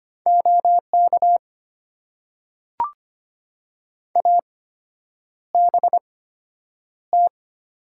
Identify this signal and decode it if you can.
Morse code, heard as a keyed tone, about 25 wpm, spaced as for 9 wpm, 690 Hz. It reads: OK ABT
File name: Top 90 QSO Elements - 8x Character Spacing - 25wpm